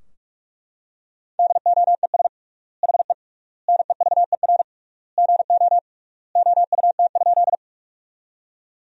Morse code, heard as a keyed tone, about 42 wpm, 700 Hz. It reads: DOES HE NEVER GO OUT?
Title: 3x Word-Spacing - Sentences from Top 300 Words - 45wpm